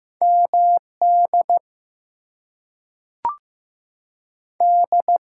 Morse code, heard as a keyed tone, about 15 wpm, 700 Hz. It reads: MD D
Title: US States and Canadian Provinces 15wpm